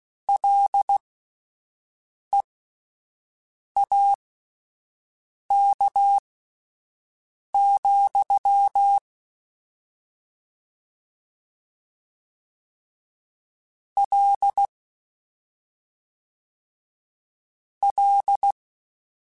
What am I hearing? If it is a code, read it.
Morse code, heard as a keyed tone, about 16 wpm, spaced as for 5 wpm, 800 Hz. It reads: LEAK, L L